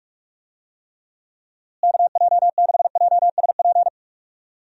Morse code, heard as a keyed tone, about 46 wpm, 700 Hz. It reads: KJ6JSP
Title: USA Call Signs - 6 Characters - Repeated 2x - 45wpm